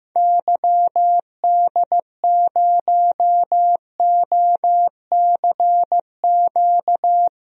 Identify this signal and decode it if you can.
Morse code, heard as a keyed tone, about 15 wpm, 700 Hz. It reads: YD0OCQ
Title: DXCC-100 Call Signs - 6 Characters - Repeated 3x - 15wpm